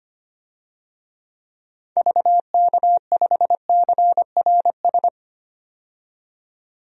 Morse code, heard as a keyed tone, about 25 wpm, 700 Hz. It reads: VK5CRS
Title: DXCC-100 Call Signs - 6 Characters - 25wpm